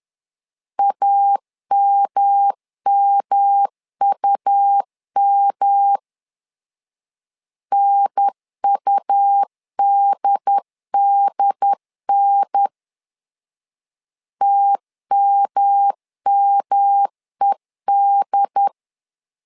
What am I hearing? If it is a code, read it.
Morse code, heard as a keyed tone, about 11 wpm, 800 Hz. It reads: AMMUM NUDDN TMMED
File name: LEZIONI / 3 MWRO / 3 M 10wpm 1828